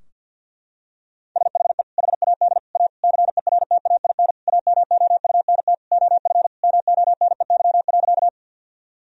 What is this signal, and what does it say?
Morse code, heard as a keyed tone, about 50 wpm, 700 Hz. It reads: SHE HAD A CERTAIN AMOUNT OF MONEY.
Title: Sentences from Top 1000 Words - Review Part 4 - 50wpm